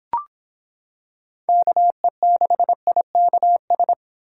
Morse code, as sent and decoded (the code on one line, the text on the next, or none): -.- . -.... .. -.- ...
KE6IKS